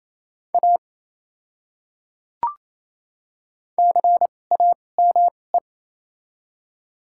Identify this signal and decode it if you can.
Morse code, heard as a keyed tone, about 27 wpm, spaced as for 20 wpm, 700 Hz. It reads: A CAME
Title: 2x Character Spacing - Top 200 Words - 28wpm